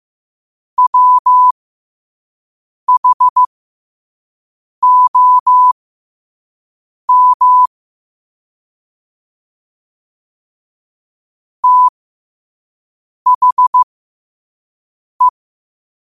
Morse code, heard as a keyed tone, about 15 wpm, spaced as for 5 wpm, 1000 Hz. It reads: WHOM THE
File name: qotdfiles / 5wpm